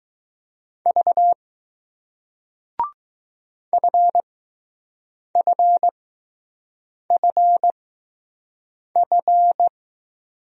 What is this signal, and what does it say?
Morse code, heard as a keyed tone, about 20 wpm, spaced as for 6 wpm, 700 Hz. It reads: V FFFF